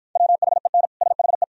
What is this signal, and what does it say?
Morse code, heard as a keyed tone, about 53 wpm, 690 Hz. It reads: WHEN SHE